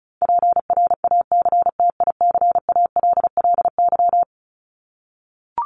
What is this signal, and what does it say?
Morse code, heard as a keyed tone, about 34 wpm, 700 Hz. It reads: PRACTICALLY
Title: LY Suffix - Top 100 Words with Base Words 35wpm